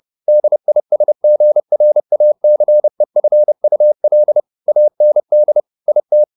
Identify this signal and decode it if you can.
Morse code, heard as a keyed tone, about 30 wpm, 600 Hz. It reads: DISGRACEFUL AND IT
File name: the old man and the sea / the old man and the sea 1 30wpm